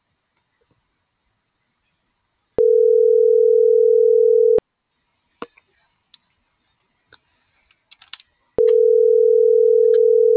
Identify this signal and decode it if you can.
Ringback tone (North America), 440 + 480 Hz.